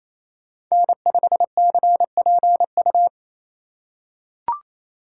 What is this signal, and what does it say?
Morse code, heard as a keyed tone, about 28 wpm, 700 Hz. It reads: N5CPU
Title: USA Call Signs - 2x Character Spacing - 28wpm